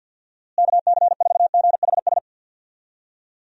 Morse code, heard as a keyed tone, about 47 wpm, 700 Hz. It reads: KC4GHS
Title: USA Call Signs - 6 Characters - Repeated 2x - 50wpm